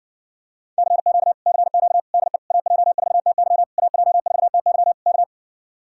Morse code, heard as a keyed tone, about 60 wpm, 700 Hz. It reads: CQ CQ DE AJ4TY AJ4TY K